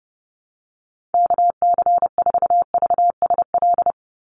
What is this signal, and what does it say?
Morse code, heard as a keyed tone, about 30 wpm, 700 Hz. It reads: KC4VSL
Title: USA Call Signs 30wpm